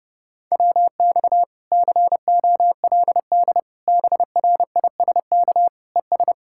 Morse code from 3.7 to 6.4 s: BRISK ES

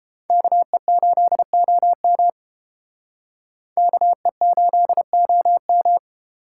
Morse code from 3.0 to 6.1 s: KE8OM

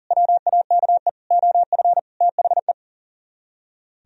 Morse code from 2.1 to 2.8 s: THE

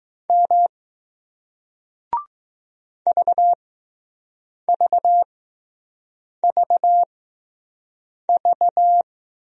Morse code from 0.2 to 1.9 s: M

